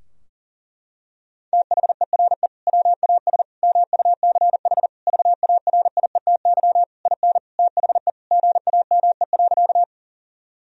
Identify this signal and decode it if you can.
Morse code, heard as a keyed tone, about 40 wpm, 700 Hz. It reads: THERE WAS MUCH VARIETY IN THE GAME.
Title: Sentences from Top 1000 Words - Review Part 4 - 40wpm